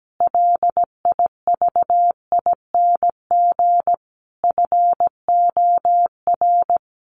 Morse code, heard as a keyed tone, about 17 wpm, 700 Hz. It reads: LIVING FOR
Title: Sentences from Top 1100 Words 17wpm